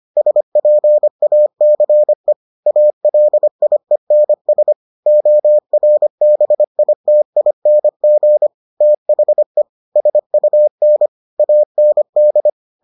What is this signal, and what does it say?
Morse code, heard as a keyed tone, about 25 wpm, 600 Hz. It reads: SPACE ALIENS ORBITING THE SUN AND